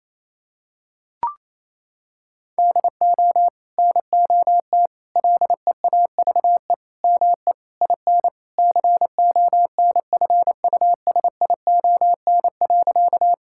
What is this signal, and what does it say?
Morse code, heard as a keyed tone, about 28 wpm, 700 Hz. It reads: DO NOT LEAVE ME IN CONFUSION.
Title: Sentences from Top 1600 Words 28wpm